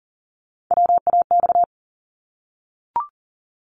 Morse code, heard as a keyed tone, about 40 wpm, 700 Hz. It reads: WAX
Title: Multi-letter Phonemes Repeated 3x 40wpm